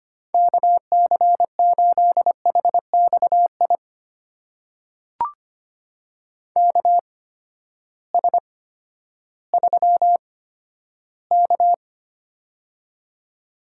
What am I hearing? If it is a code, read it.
Morse code, heard as a keyed tone, about 25 wpm, spaced as for 12 wpm, 700 Hz. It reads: KC8HXI KS3K
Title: USA Call Signs - 8x Character Spacing - 25wpm